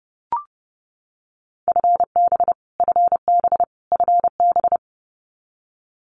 Morse code, heard as a keyed tone, about 30 wpm, 700 Hz. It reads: FB FB FB
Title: Abbreviations Repeated 3x - 30wpm